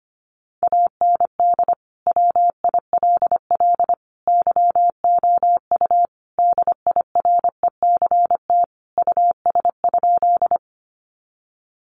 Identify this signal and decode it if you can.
Morse code, heard as a keyed tone, about 25 wpm, 700 Hz. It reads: AND WILL YOU DIRECT US?
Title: Sentences from Top 400 Words 25wpm